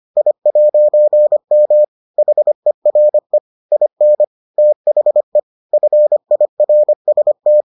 Morse code, heard as a keyed tone, about 25 wpm, 600 Hz. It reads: I'M HERE IN THE FIRST